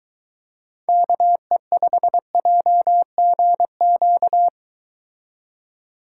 Morse code, heard as a keyed tone, about 22 wpm, 700 Hz. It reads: KE5JGQ